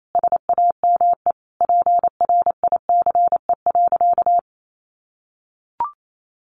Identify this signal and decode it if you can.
Morse code, heard as a keyed tone, about 28 wpm, 700 Hz. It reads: SAME PRICE.
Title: Sentences from Top 600 Words 28wpm